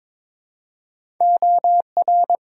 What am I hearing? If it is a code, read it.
Morse code, heard as a keyed tone, about 22 wpm, 700 Hz. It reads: OR
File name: practice / 22wpm / Build A Phrase - 2-Word Phrases - 22wpm